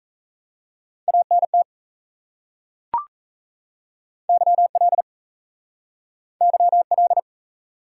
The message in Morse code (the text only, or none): ANT YL YL